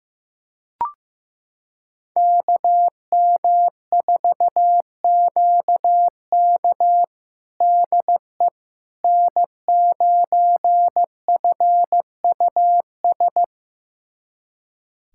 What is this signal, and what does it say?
Morse code, heard as a keyed tone, about 16 wpm, 700 Hz. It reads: KM4QK DE N9FUS
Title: USA Callsign DE Callsign - 5 Characters - 15wpm